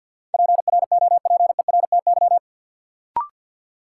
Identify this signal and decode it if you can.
Morse code, heard as a keyed tone, about 50 wpm, 700 Hz. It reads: PROPERTY
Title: Sentences from Top 1000 Words - New Word Drill - 50wpm